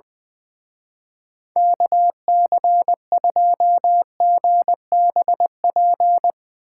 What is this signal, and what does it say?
Morse code, heard as a keyed tone, about 22 wpm, 700 Hz. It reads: KC2GBP
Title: Speed Racing - USA 2x3 Call Signs 20wpm